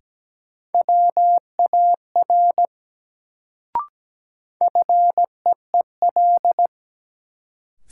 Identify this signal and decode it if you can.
Morse code, heard as a keyed tone, about 17 wpm, 700 Hz. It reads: WAR FEEL
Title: Sentences from Top 400 Words - New Word Drill - 17wpm